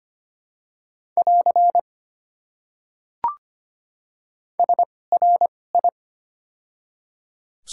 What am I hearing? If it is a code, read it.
Morse code, heard as a keyed tone, about 25 wpm, spaced as for 18 wpm, 700 Hz. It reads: <AR> SRI
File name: Top 90 QSO Elements - 2x Character Spacing - 25wpm